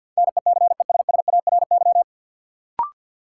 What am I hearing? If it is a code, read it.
Morse code, heard as a keyed tone, about 53 wpm, 700 Hz. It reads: NECESSARY
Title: Sentences from Top 800 Words - New Word Drill - 50wpm